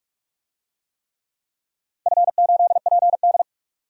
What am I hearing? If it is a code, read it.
Morse code, heard as a keyed tone, about 44 wpm, 700 Hz. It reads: F8PD
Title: DXCC-100 Call Signs - 4 Characters - 45wpm